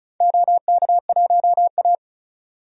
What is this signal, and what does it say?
Morse code, heard as a keyed tone, about 36 wpm, 690 Hz. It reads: OK1A
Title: DXCC-100 Call Signs - 4 Characters - 35wpm